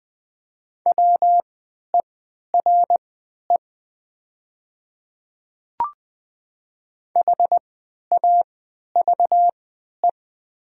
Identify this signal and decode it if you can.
Morse code, heard as a keyed tone, about 20 wpm, spaced as for 11 wpm, 700 Hz. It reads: WERE HAVE
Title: 3x Character Spacing - Top 100 Words - 20wpm